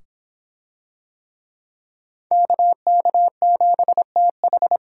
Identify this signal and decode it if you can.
Morse code, heard as a keyed tone, about 24 wpm, 700 Hz. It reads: KK7TH